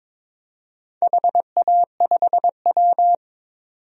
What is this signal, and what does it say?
Morse code, heard as a keyed tone, about 22 wpm, 700 Hz. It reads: HA5W
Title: DXCC-100 Call Signs - 4 Characters - 22wpm